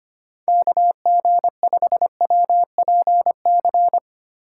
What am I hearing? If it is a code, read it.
Morse code, heard as a keyed tone, about 25 wpm, 700 Hz. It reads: KG5WPC
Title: USA Call Signs - 6 Characters - 4x Character Spacing - 25wpm